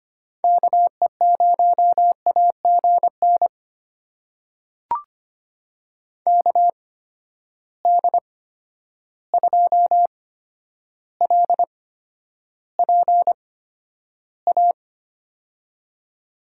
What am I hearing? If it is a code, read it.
Morse code, heard as a keyed tone, about 25 wpm, spaced as for 11 wpm, 700 Hz. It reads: KE0AGN KD2LPA
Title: USA Call Signs - 6 Characters - 8x Character Spacing - 25wpm